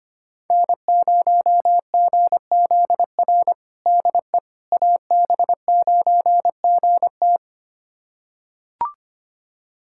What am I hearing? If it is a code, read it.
Morse code, heard as a keyed tone, about 25 wpm, 700 Hz. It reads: N0GZR DE AB9GT
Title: USA Callsign DE Callsign - 5 Characters - 25wpm